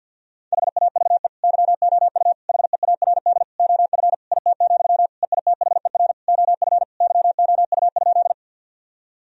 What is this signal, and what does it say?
Morse code, heard as a keyed tone, about 50 wpm, 700 Hz. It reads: HAVE YOU HEARD OF IT, EITHER OF YOU?
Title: Sentences from Top 1000 Words - Review Part 4 - 50wpm